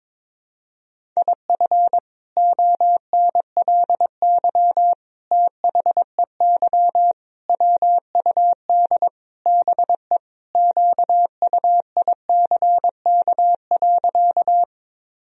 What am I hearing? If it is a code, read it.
Morse code, heard as a keyed tone, about 22 wpm, 700 Hz. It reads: IF ONLY THEY WUD BE QUICK.
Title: Sentences from Top 500 Words and Abbreviations 22wpm